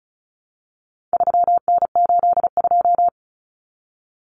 Morse code, heard as a keyed tone, about 34 wpm, 700 Hz. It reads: 3N82
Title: Common Cut Numbers - 4-Digit 35wpm